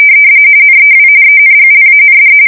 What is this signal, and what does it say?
RTTY, about 45.45 baud. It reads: NEW ENGLAND
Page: 170 Shift 2125/2295